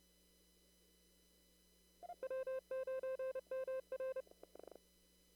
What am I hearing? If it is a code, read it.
Morse code, heard as a keyed tone, about 31 wpm, 520 Hz. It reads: W9MR
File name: CW / kt5j wpx cw-op-k5pi